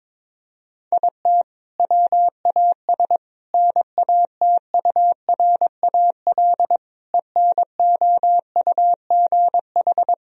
Morse code, heard as a keyed tone, about 22 wpm, 700 Hz. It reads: IT WAS NATURAL ENOUGH
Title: Sentences from Top 1000 Words - Review Part 4 - 22wpm